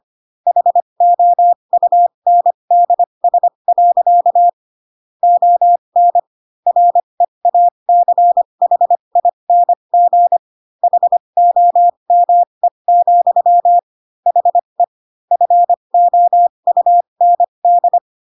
Morse code, heard as a keyed tone, about 25 wpm, spaced as for 22 wpm, 700 Hz. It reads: HOUNDS. ON REACHING HOME, HE FOUND
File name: Pinocchio / Pinocchio ch4 18 25 ik1hge